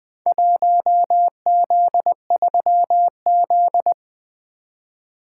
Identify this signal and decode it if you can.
Morse code, heard as a keyed tone, about 20 wpm, 700 Hz. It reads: 1Z3Z
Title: Four Letter-Number Combinations - 20wpm